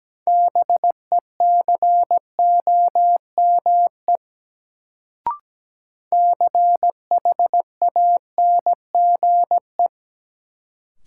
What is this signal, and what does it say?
Morse code, heard as a keyed tone, about 17 wpm, 700 Hz. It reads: BECOME CHANGE
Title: Sentences from Top 600 Words - New Word Drill - 17wpm